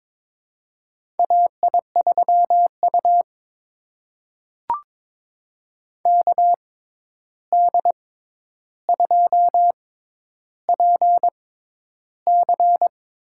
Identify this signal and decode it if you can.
Morse code, heard as a keyed tone, about 22 wpm, spaced as for 11 wpm, 700 Hz. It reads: AI3U KD2PC
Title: USA Call Signs - 6x Character Spacing - 22wpm